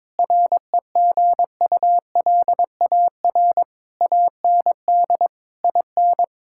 Morse code, heard as a keyed tone, about 22 wpm, 700 Hz. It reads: REGULAR AND IN
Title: Sentences from Top 1000 Words - Review Part 6 - 22wpm